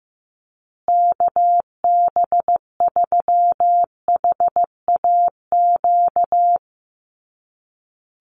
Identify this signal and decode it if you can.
Morse code, heard as a keyed tone, about 15 wpm, 700 Hz. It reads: KB3HAQ
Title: USA Call Signs Repeated 2x 15wpm